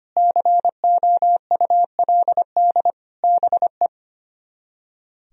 Morse code, heard as a keyed tone, about 33 wpm, 700 Hz. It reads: COULD BE